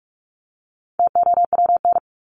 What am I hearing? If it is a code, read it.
Morse code, heard as a keyed tone, about 44 wpm, 700 Hz. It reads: TOWN
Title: Top 500 Words 45wpm